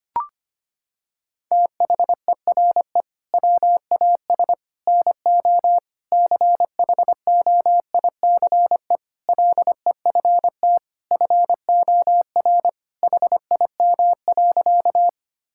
Morse code, heard as a keyed tone, about 25 wpm, 700 Hz. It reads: THERE WAS NO CHOICE LEFT FOR HIM.
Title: Sentences from Top 1000 Words - Review Part 6 - 25wpm